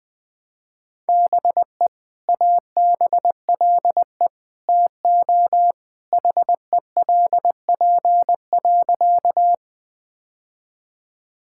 Morse code, heard as a keyed tone, about 20 wpm, 700 Hz. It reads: BE ABLE TO HELP.
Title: Sentences with 2-Word Phrases - 20wpm